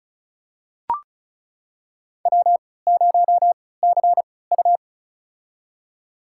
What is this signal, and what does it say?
Morse code, heard as a keyed tone, about 35 wpm, spaced as for 20 wpm, 690 Hz. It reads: W0CU
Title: USA Call Signs - 4 Characters - 3x Character Spacing - 35wpm